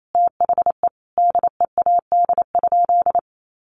Morse code, heard as a keyed tone, about 28 wpm, 700 Hz. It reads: THE DEAD?
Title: Sentences from Top 1500 Words 28wpm